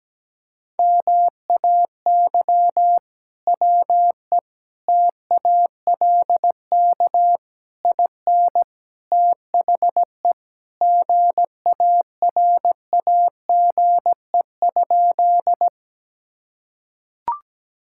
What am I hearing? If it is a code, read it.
Morse code, heard as a keyed tone, about 17 wpm, 700 Hz. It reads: MAY WE TALK IN THE GARAGE?
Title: Sentences from Top 1600 Words 17wpm